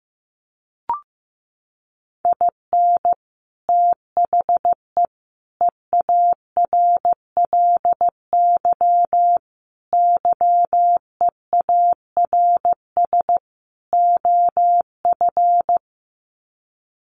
Morse code, heard as a keyed tone, about 15 wpm, 700 Hz. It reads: IN THE EARLY YEARS OF
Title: Sets of 5 Words Encore 15wpm